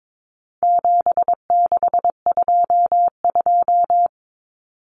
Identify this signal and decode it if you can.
Morse code, heard as a keyed tone, about 22 wpm, 700 Hz. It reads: 7622
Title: Common Cut Numbers - 4-Digit 22wpm